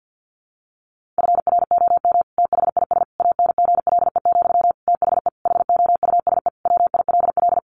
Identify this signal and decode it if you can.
Morse code, heard as a keyed tone, about 48 wpm, 700 Hz. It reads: FROM THIS ANGLE, THE HOUSE WILL
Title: Sentences from Top 1500 Words 50wpm